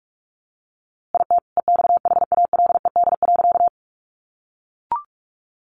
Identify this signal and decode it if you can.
Morse code, heard as a keyed tone, about 46 wpm, 700 Hz. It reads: IT EXHALED.
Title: A Sound Of Thunder 45wpm